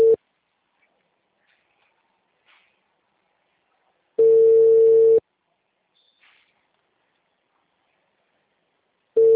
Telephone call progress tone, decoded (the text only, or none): ringback tone